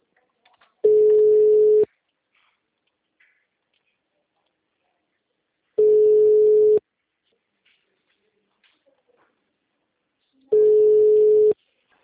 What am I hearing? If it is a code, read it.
Ringback tone (Europe, 425 Hz).
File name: out- -2017- - - .54292